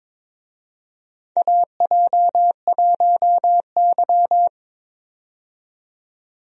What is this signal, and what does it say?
Morse code, heard as a keyed tone, about 22 wpm, 700 Hz. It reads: AJ1Y
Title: USA Call Signs - 4 Characters - 22wpm